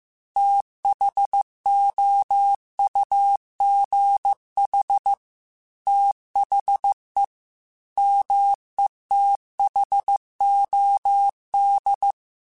Morse code, heard as a keyed tone, about 15 wpm, 800 Hz. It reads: THOUGH THE METHOD
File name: Technical Subjects / Technical Subjects-mp3s / AND Gate @13